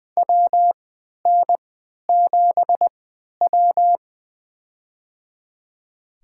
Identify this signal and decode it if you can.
Morse code, heard as a keyed tone, about 20 wpm, spaced as for 11 wpm, 700 Hz. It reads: WN7W
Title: USA Call Signs - 4 Characters - 3x Character Spacing - 20wpm